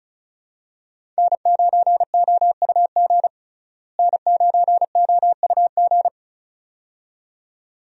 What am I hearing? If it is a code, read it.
Morse code, heard as a keyed tone, about 36 wpm, 700 Hz. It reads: N9OUG N9OUG
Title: USA Call Signs - 5 Characters - Repeated 2x - 35wpm